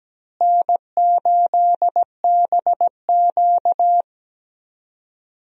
Morse code, heard as a keyed tone, about 17 wpm, 700 Hz. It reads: N8BQ
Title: USA Call Signs - 4 Characters - Repeated 3x - 17wpm